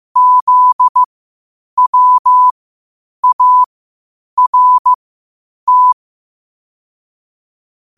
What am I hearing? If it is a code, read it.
Morse code, heard as a keyed tone, about 15 wpm, spaced as for 8 wpm, 1000 Hz. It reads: ZWART
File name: qotdfiles / 8wpm